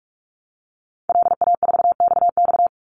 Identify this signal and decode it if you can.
Morse code, heard as a keyed tone, about 44 wpm, 700 Hz. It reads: LA4XX
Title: Call Signs Repeated 2x - 45wpm